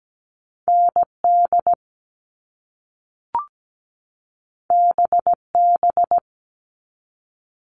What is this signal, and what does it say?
Morse code, heard as a keyed tone, about 17 wpm, 700 Hz. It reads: ND BB
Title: All Cut Numbers - 2-Digit 17wpm